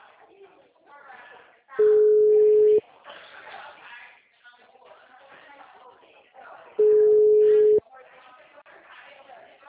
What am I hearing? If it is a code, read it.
Ringback tone (Europe, 425 Hz), with speech.